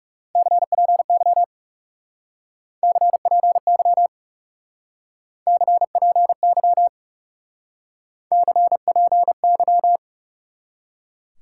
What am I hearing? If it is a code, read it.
Morse code, heard as a keyed tone, about 40 wpm, 700 Hz. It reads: CPY CPY CPY CPY